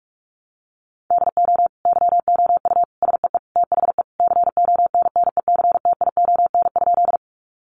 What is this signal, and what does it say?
Morse code, heard as a keyed tone, about 44 wpm, 700 Hz. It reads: DO YOU SEE THE CONNECTION?
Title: Sentences from Top 1300 Words 45wpm